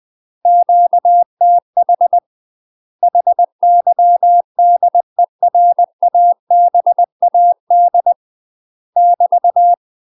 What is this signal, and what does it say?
Morse code, heard as a keyed tone, about 20 wpm, 700 Hz. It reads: QTH HYDERABAD <BT>
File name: UK MASTER QSOs TRAINING / 20WPM / UK MASTER QSOs 0039